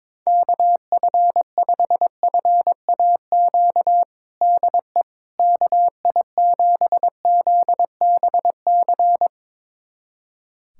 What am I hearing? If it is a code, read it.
Morse code, heard as a keyed tone, about 22 wpm, 700 Hz. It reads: KF5FAQ DE KI7ZBC
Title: USA Callsign DE Callsign 22wpm